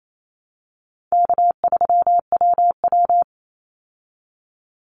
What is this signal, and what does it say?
Morse code, heard as a keyed tone, about 28 wpm, 700 Hz. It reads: K3WW
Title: Easy Call Signs Repeated 2x - 28wpm